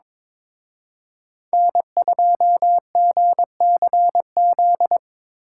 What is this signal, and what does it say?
Morse code, heard as a keyed tone, about 26 wpm, 700 Hz. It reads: N2GCZ
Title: Speed Racing - USA 1x3 Call Signs 22wpm